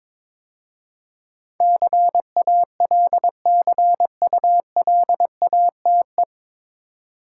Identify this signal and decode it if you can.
Morse code, heard as a keyed tone, about 21 wpm, 700 Hz. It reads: CALCULATE
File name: 9 Letter Words - First 3 Letters Given - 22wpm